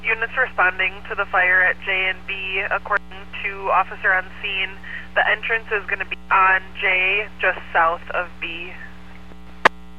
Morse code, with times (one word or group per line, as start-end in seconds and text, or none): none